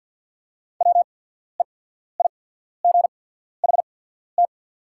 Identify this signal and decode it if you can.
Morse code, heard as a keyed tone, about 48 wpm, spaced as for 14 wpm, 690 Hz. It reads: WEIGHT
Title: 8x Character Spacing - Top 500 Words - 50wpm